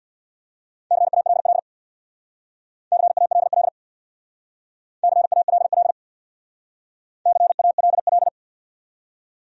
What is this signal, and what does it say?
Morse code, heard as a keyed tone, about 68 wpm, 700 Hz. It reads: CALL CALL CALL CALL